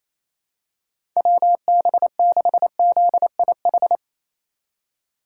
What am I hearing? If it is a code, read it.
Morse code, heard as a keyed tone, about 28 wpm, 700 Hz. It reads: WB6ZIH
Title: USA Call Signs - 6 Characters - 3x Character Spacing - 28wpm